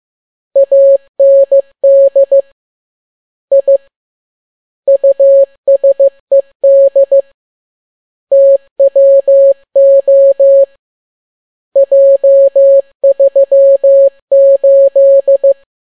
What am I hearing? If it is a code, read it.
Morse code, heard as a keyed tone, about 15 wpm, 550 Hz. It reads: AND I USED TWO J38
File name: n6tt p1 15C13